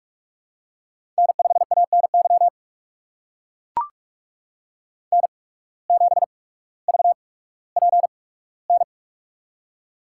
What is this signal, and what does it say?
Morse code, heard as a keyed tone, about 44 wpm, spaced as for 19 wpm, 700 Hz. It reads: N5ANY N7VPN
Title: USA Call Signs - 5 Characters - 8x Character Spacing - 45wpm